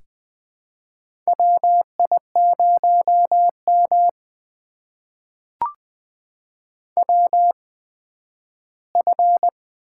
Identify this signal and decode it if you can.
Morse code, heard as a keyed tone, about 20 wpm, spaced as for 12 wpm, 700 Hz. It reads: WI0M WF
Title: USA Call Signs - 4 Characters - 8x Character Spacing - 20wpm